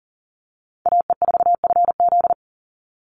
Morse code, heard as a keyed tone, about 40 wpm, 700 Hz. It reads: AE4FZ
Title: NA SOTA Chaser Callsigns 40wpm